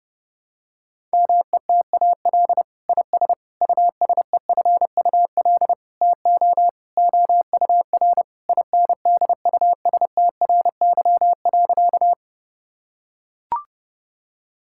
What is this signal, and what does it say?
Morse code, heard as a keyed tone, about 30 wpm, 700 Hz. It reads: METAL IS USEFUL TO OUR INDUSTRY.
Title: Sentences from Top 1000 Words - Review Part 6 - 30wpm